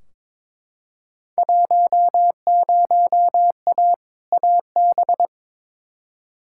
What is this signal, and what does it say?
Morse code, heard as a keyed tone, about 21 wpm, 700 Hz. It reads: 10A AB
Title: Field Day Exchanges Encore - 22wpm